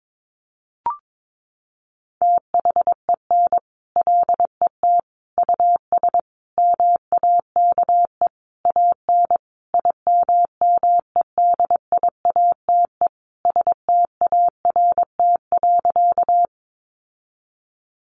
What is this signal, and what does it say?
Morse code, heard as a keyed tone, about 22 wpm, 700 Hz. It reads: THEN LET US MAKE AN IMMEDIATE START.
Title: Sentences from Top 1200 Words 22wpm